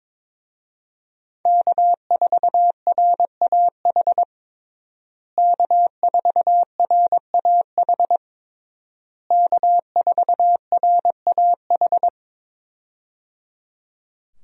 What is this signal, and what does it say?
Morse code, heard as a keyed tone, about 21 wpm, 700 Hz. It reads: K4RAH K4RAH K4RAH
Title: USA Call Signs - 5 Characters - Repeated 3x - 22wpm